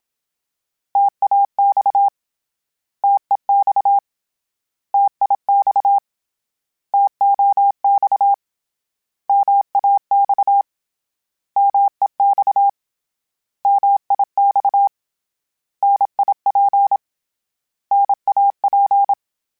Morse code, heard as a keyed tone, about 27 wpm, 800 Hz. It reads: TAX TEX TIX TOX MAX MEX MIX NIP NAP
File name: Contrasts / Contrasts-mp3s / 3LtrContrasts @25